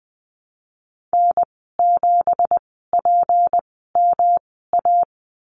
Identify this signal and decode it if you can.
Morse code, heard as a keyed tone, about 20 wpm, spaced as for 15 wpm, 700 Hz. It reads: N7PMA
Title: 186 Lesson 30g - Letter b - Call Signs 20wpm